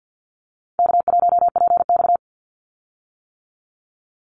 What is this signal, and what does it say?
Morse code, heard as a keyed tone, about 50 wpm, 700 Hz. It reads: K1PX
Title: Easy Call Signs - 50wpm